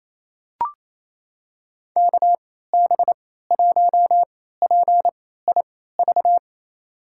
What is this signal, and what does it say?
Morse code, heard as a keyed tone, about 28 wpm, spaced as for 16 wpm, 700 Hz. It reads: KB1PIV
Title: USA Call Signs - 6 Characters - 3x Character Spacing - 28wpm